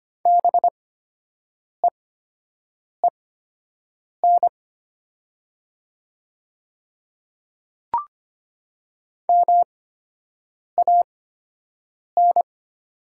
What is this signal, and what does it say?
Morse code, heard as a keyed tone, about 25 wpm, spaced as for 7 wpm, 700 Hz. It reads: BEEN MAN
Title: 8x Character Spacing - Top 200 Words - 25wpm